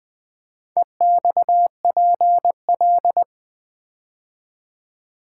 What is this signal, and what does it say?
Morse code, heard as a keyed tone, about 20 wpm, 700 Hz. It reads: EXPL
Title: Build A Word - 4 Letters At A Time - 12-Letter Words - 20wpm